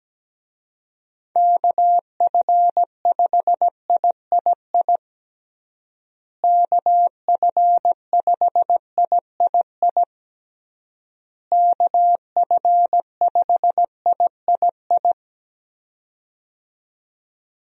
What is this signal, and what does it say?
Morse code, heard as a keyed tone, about 17 wpm, 700 Hz. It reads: KF5III KF5III KF5III
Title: USA Call Signs - 6 Characters - Repeated 3x - 17wpm